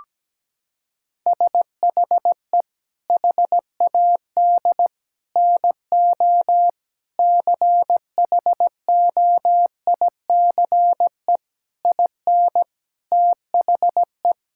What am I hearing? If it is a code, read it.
Morse code, heard as a keyed tone, about 17 wpm, 700 Hz. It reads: SHE HAD NO CHOICE IN THE
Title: Sentences from Top 1000 Words - Review Part 6 - 17wpm